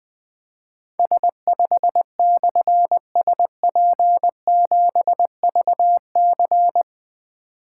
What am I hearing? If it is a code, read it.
Morse code, heard as a keyed tone, about 21 wpm, 700 Hz. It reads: S5/SP7VC
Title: Difficult Call Signs 20wpm